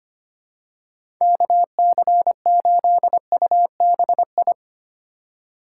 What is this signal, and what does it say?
Morse code, heard as a keyed tone, about 25 wpm, 700 Hz. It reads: KC8UBI
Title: USA Call Signs - 6 Characters - 2x Character Spacing - 25wpm